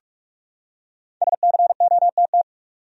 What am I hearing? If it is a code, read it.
Morse code, heard as a keyed tone, about 42 wpm, 700 Hz. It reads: SCOTT
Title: USA Common Names - Max 5 Letters - 45wpm